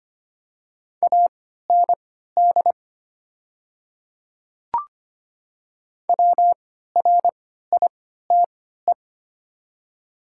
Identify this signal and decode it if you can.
Morse code, heard as a keyed tone, about 25 wpm, spaced as for 14 wpm, 700 Hz. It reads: AND WRITE
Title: 3x Character Spacing - Top 100 Words - 25wpm